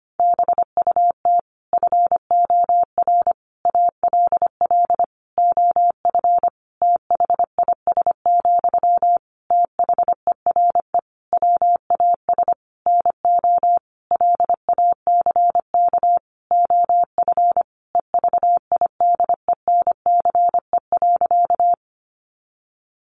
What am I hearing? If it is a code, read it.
Morse code, heard as a keyed tone, about 25 wpm, 700 Hz. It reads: BUT FOR ALL OF THIS, THERE WAS NO LACK OF EVIDENCE.
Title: Sentences from Top 1300 Words 25wpm